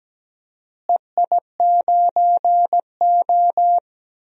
Morse code, heard as a keyed tone, about 17 wpm, 700 Hz. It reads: EI9O
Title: DXCC-100 Call Signs - 4 Characters - Repeated 3x - 17wpm